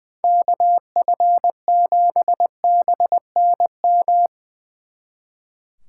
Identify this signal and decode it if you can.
Morse code, heard as a keyed tone, about 22 wpm, 700 Hz. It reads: KF7BNM
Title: Speed Racing - USA 2x3 Call Signs 20wpm